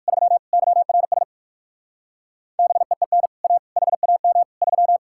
Morse code, heard as a keyed tone, about 58 wpm, 700 Hz. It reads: 3 YRS BEEN A HAM 3